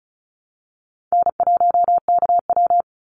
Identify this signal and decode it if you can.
Morse code, heard as a keyed tone, about 34 wpm, 700 Hz. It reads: N1KW
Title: Easy Call Signs Repeated 3x - 35wpm